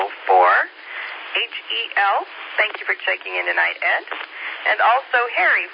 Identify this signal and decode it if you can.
Emergency Alert System (SAME) tones, heard as harsh buzzing bursts, with speech. It is heard from 0.6 to 4.7 s.